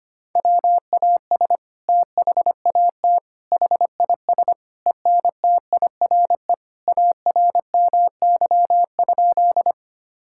Morse code, heard as a keyed tone, about 25 wpm, 700 Hz. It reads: WAS THAT HIS ENTIRE ARMY?
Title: Sentences from Top 1000 Words - Review Part 5 - 25wpm